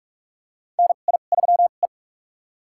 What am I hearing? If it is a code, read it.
Morse code, heard as a keyed tone, about 46 wpm, spaced as for 32 wpm, 700 Hz. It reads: NI3E
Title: USA Call Signs - 4 Characters - 2x Character Spacing - 45wpm